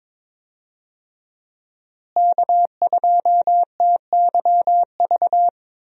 Morse code, heard as a keyed tone, about 22 wpm, 700 Hz. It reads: K2TYV
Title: USA Call Signs - 5 Characters - Repeated 2x - 22wpm